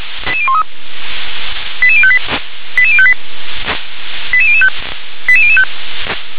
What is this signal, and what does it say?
Five-tone selective call (ZVEI), heard as quick runs of five tones, tones of about 70 ms. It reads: 88013 88058 88058 88005 88005